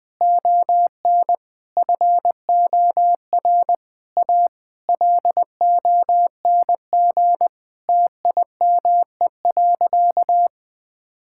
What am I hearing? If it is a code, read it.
Morse code, heard as a keyed tone, about 20 wpm, 700 Hz. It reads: ON FOR A LONG TIME.
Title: Sentences with 8-Word Phrases - 20wpm